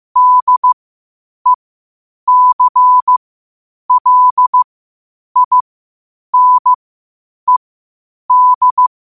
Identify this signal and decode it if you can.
Morse code, heard as a keyed tone, about 15 wpm, spaced as for 8 wpm, 1000 Hz. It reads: DECLINED
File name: qotdfiles / 8wpm